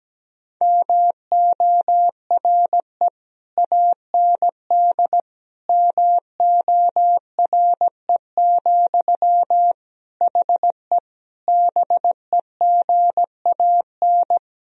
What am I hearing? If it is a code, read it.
Morse code, heard as a keyed tone, about 17 wpm, 700 Hz. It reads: MORE AND MORE, HE BEGAN
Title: Sentences from Top 1000 Words - Review Part 3 - 17wpm